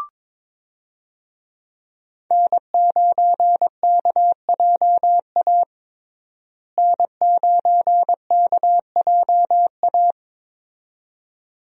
Morse code, heard as a keyed tone, about 22 wpm, 700 Hz. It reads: N9KJA N9KJA
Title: USA Call Signs - 5 Characters - Repeated 2x - 22wpm